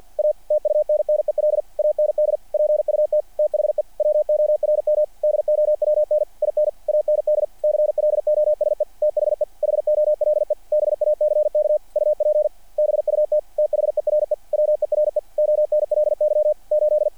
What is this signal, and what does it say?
Morse code, heard as a keyed tone, about 48 wpm, 600 Hz. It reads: A TUNNEL AND PUT THE WORM DOWN IN AND CLOSE THE HOLE BACK UP BUT THERE WERE ONLY 8
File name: archived / CRSnet